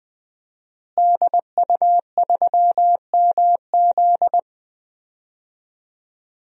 Morse code, heard as a keyed tone, about 20 wpm, 700 Hz. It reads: DU3MZ
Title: DXCC-100 Call Signs - 5 Characters - Repeated 2x - 20wpm